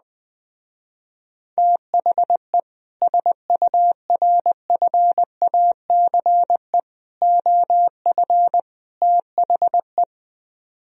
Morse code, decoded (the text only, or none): THE SURFACE OF THE